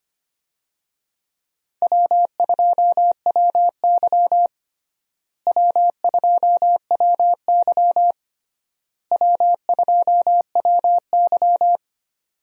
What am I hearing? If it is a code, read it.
Morse code, heard as a keyed tone, about 25 wpm, 700 Hz. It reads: W2WY W2WY W2WY
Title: USA Call Signs - 4 Characters - Repeated 3x - 25wpm